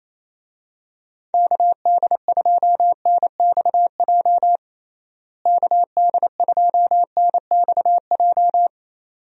Morse code, heard as a keyed tone, about 28 wpm, 700 Hz. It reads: KD2NXJ KD2NXJ
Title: USA Call Signs - 6 Characters - Repeated 2x - 28wpm